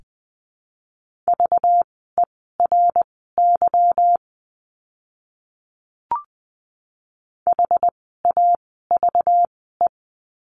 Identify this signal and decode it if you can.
Morse code, heard as a keyed tone, about 20 wpm, spaced as for 14 wpm, 700 Hz. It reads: VERY HAVE
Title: 230 Lesson 37f - Letter x - Words 20wpm